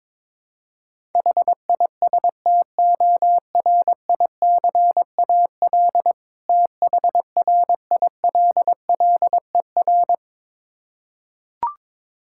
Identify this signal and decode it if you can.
Morse code, heard as a keyed tone, about 22 wpm, 690 Hz. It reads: HISTORICAL THRILLER
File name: Difficult Words with Hints - 22wpm